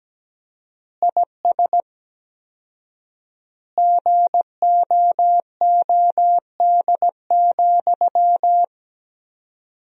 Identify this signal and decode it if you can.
Morse code, heard as a keyed tone, about 17 wpm, 700 Hz. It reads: IS GOOD,
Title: 4x Word-Spacing - Sentences from Top 200 Words - 17wpm